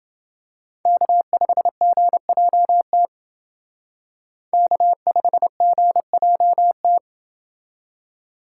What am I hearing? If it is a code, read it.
Morse code, heard as a keyed tone, about 28 wpm, 700 Hz. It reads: K5GJT K5GJT